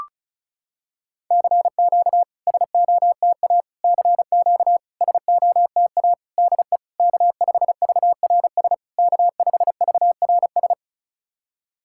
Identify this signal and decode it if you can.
Morse code, heard as a keyed tone, about 36 wpm, 700 Hz. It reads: CQ SOTA CQ SOTA DE K5VRS K5VRS
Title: SOTA QSO Practice 35wpm